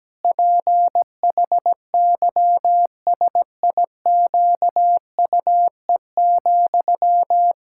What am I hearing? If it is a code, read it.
Morse code, heard as a keyed tone, about 17 wpm, 700 Hz. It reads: PHYSIQUE,
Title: Pangram Challenge v2 - 17wpm